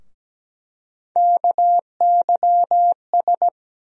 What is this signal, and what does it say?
Morse code, heard as a keyed tone, about 17 wpm, 700 Hz. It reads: KYS
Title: USA Call Sign Postfixes - 3 Letters 17wpm